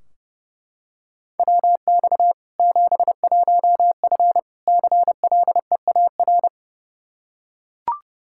Morse code, heard as a keyed tone, about 30 wpm, 700 Hz. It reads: WX 71F CLEAR
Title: K1USN SST QSO Practice 30wpm